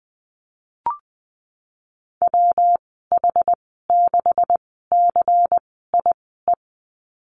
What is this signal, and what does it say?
Morse code, heard as a keyed tone, about 20 wpm, spaced as for 14 wpm, 700 Hz. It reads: WH6CIE
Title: 200 Lesson 32g - Letter k - Call Signs 20wpm